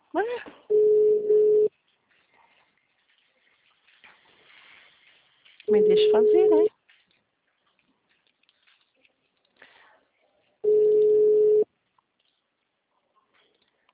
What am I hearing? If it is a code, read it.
Ringback tone (Europe, 425 Hz), with speech.